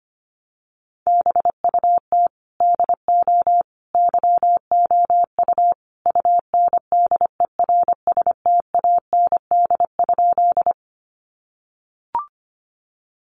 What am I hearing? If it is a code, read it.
Morse code, heard as a keyed tone, about 25 wpm, 700 Hz. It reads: BUT DO YOU UNDERSTAND?
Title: Sentences from Top 600 Words 25wpm